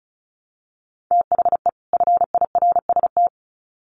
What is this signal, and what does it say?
Morse code, heard as a keyed tone, about 34 wpm, 700 Hz. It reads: THE FIRST
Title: Sets of 2 Words 35wpm